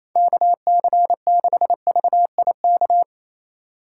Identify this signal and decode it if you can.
Morse code, heard as a keyed tone, about 37 wpm, 700 Hz. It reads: KC6VIK